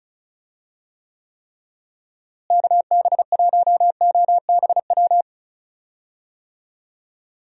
Morse code, heard as a keyed tone, about 34 wpm, 700 Hz. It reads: KB1OBW
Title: USA Call Signs - 6 Characters - Repeated 3x - 35wpm